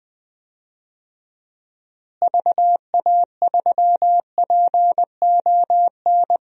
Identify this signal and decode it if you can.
Morse code, heard as a keyed tone, about 19 wpm, 700 Hz. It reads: VA3PON
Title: DXCC-100 Call Signs - 6 Characters - Repeated 3x - 20wpm